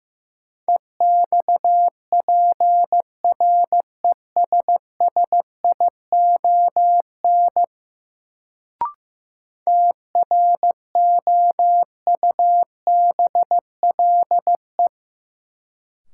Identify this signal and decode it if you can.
Morse code, heard as a keyed tone, about 15 wpm, 700 Hz. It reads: EXPRESSION TROUBLE
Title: Sentences from Top 1200 Words - New Word Drill - 15wpm